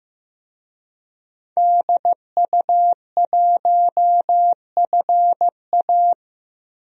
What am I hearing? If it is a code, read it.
Morse code, heard as a keyed tone, about 15 wpm, 700 Hz. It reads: DU1FA
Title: DXCC-100 Call Signs - 5 Characters - 15wpm